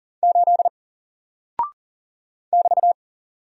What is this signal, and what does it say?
Morse code, heard as a keyed tone, about 38 wpm, 700 Hz. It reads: 8 <BT>
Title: Single Character (Letter:Number:Punctuation:Prosign) Alt 40wpm